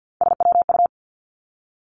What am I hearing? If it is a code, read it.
Morse code, heard as a keyed tone, about 50 wpm, 700 Hz. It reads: SWU
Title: Three Challenging Letter-Number Combinations 50wpm